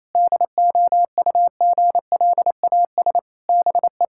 Morse code, heard as a keyed tone, about 28 wpm, 700 Hz. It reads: DOUGLAS BE
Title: Pangram Challenge v2 - 28wpm